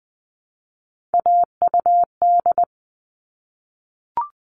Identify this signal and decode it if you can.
Morse code, heard as a keyed tone, about 20 wpm, 700 Hz. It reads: AUD
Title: All Cut Numbers - 3-Digit 20wpm